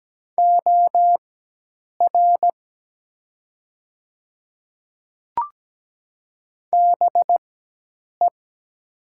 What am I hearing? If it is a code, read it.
Morse code, heard as a keyed tone, about 17 wpm, spaced as for 8 wpm, 700 Hz. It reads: OR BE
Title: 4x Character Spacing - Top 200 Words - 17wpm